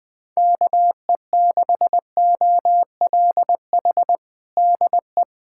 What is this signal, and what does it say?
Morse code, heard as a keyed tone, about 20 wpm, 700 Hz. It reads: KE6OLH DE
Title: USA Callsign DE Callsign - 6 Characters - 20wpm